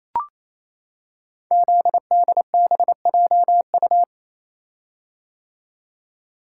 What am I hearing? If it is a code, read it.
Morse code, heard as a keyed tone, about 27 wpm, 700 Hz. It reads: ZDBJU
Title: Code Groups - 5 Letters - 28wpm